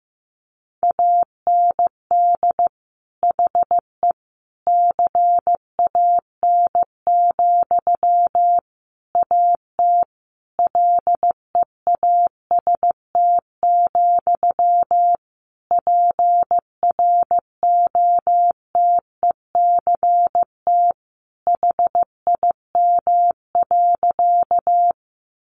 Morse code, heard as a keyed tone, about 14 wpm, 700 Hz. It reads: AND HE CAN, AT LEAST, PROTECT HIM.
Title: Sentences from Top 900 Words 15wpm